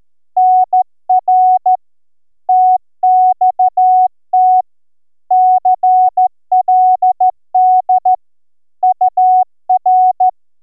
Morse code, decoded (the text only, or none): NR TXT CLD UR